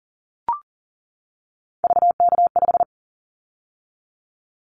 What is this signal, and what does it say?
Morse code, heard as a keyed tone, about 41 wpm, 700 Hz. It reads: VK5
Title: Callsign Prefixes for Top 50 DXCC - 40wpm